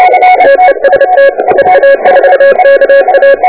Morse code, heard as a keyed tone, about 30 wpm, 1650 Hz, with noise. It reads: TEST EA4KA
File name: oh1noa / ct8t 2003 / k9pg-ct8t 20